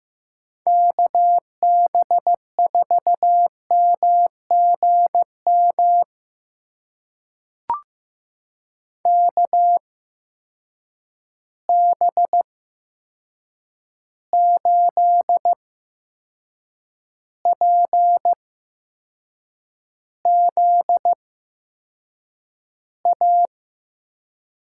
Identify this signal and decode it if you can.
Morse code, heard as a keyed tone, about 15 wpm, spaced as for 6 wpm, 700 Hz. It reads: KB4MGM KB8PZA
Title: USA Call Signs - 6 Characters - 8x Character Spacing - 15wpm